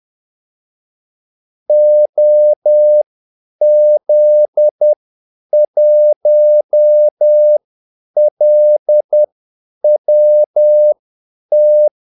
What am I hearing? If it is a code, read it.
Morse code, heard as a keyed tone, about 10 wpm, spaced as for 8 wpm, 600 Hz. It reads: OZ1LWT
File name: Callsigns-10-8wpm